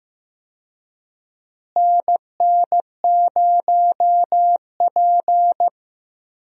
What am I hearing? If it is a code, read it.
Morse code, heard as a keyed tone, about 15 wpm, 700 Hz. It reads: NN0P
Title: USA Call Signs - 4 Characters - Repeated 3x - 15wpm